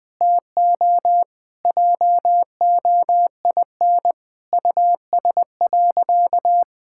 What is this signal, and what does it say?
Morse code, heard as a keyed tone, about 20 wpm, 700 Hz. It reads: TO JOIN US.
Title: Sentences with 5-Word Phrases - 20wpm